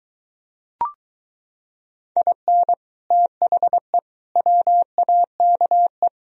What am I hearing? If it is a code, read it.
Morse code, heard as a keyed tone, about 23 wpm, 700 Hz. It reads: IN THE WAKE